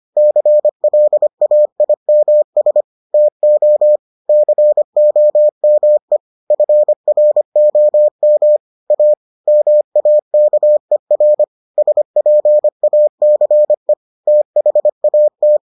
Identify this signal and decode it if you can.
Morse code, heard as a keyed tone, about 25 wpm, 600 Hz. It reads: CLAIMS TO COME FROM A MAKER SPACE THAT